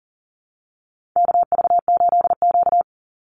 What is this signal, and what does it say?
Morse code, heard as a keyed tone, about 40 wpm, 700 Hz. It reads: KV8Q
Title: Call Signs - 40wpm